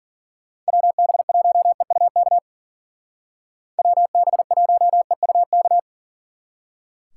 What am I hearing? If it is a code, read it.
Morse code, heard as a keyed tone, about 49 wpm, 700 Hz. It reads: WB1EUK WB1EUK
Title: Speed Racing - USA 2x3 Call Signs 40wpm